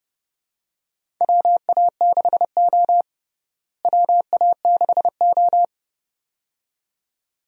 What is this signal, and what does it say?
Morse code, heard as a keyed tone, about 30 wpm, 700 Hz. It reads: WA6O WA6O
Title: USA Call Signs - 4 Characters - Repeated 2x - 30wpm